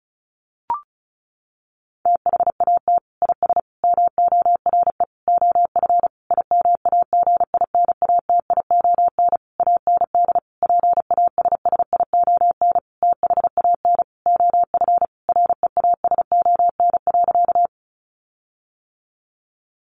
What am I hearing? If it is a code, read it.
Morse code, heard as a keyed tone, about 34 wpm, 700 Hz. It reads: THAT IS MORE OF IMAGINATION AND PASSION THAN OF REASON.
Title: Sentences from Top 1300 Words 35wpm